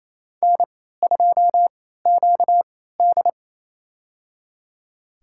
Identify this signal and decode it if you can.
Morse code, heard as a keyed tone, about 28 wpm, spaced as for 16 wpm, 700 Hz. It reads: N2QD
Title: USA Call Signs - 3x Character Spacing - 28wpm